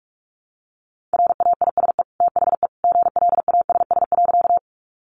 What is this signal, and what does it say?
Morse code, heard as a keyed tone, about 44 wpm, 700 Hz. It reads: RAISE THE GLASS.
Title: Sentences from Top 1400 Words 45wpm